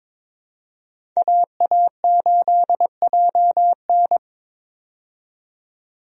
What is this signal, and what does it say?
Morse code, heard as a keyed tone, about 22 wpm, 700 Hz. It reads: AA8JN
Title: USA Call Signs - 5 Characters - 22wpm